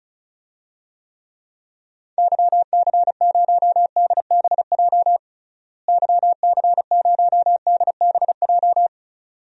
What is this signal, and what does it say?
Morse code, heard as a keyed tone, about 36 wpm, 700 Hz. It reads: YC0DBJ YC0DBJ
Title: DXCC-100 Call Signs - 6 Characters - Repeated 2x - 35wpm